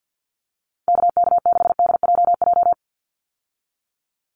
Morse code, heard as a keyed tone, about 50 wpm, 700 Hz. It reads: KK6DJJ
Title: USA Call Signs 50wpm